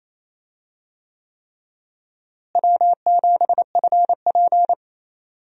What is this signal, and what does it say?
Morse code, heard as a keyed tone, about 28 wpm, 700 Hz. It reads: W7FP
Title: USA Call Signs - 4 Characters - Repeated 3x - 28wpm